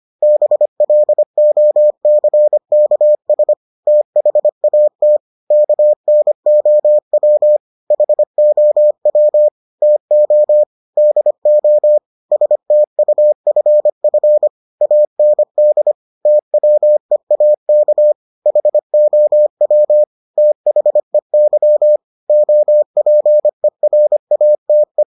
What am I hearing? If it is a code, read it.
Morse code, heard as a keyed tone, about 25 wpm, 600 Hz. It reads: BLOCKS THAT KNOW HOW TO DO STUFF AND TWEAK HOW THEY OPERATE